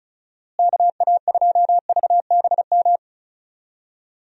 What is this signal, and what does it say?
Morse code, heard as a keyed tone, about 36 wpm, 700 Hz. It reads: KA2VBM
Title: USA Call Signs - 6 Characters - Repeated 2x - 35wpm